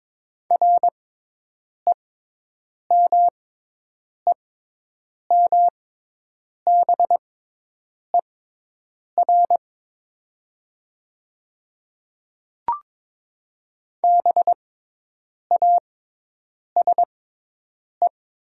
Morse code, heard as a keyed tone, about 22 wpm, spaced as for 8 wpm, 700 Hz. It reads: REMEMBER BASE
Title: 6x Character Spacing - Top 500 Words - 22wpm